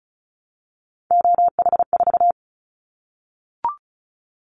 Morse code, heard as a keyed tone, about 36 wpm, 700 Hz. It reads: OH4
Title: Callsign Prefixes for Top 50 DXCC - 35wpm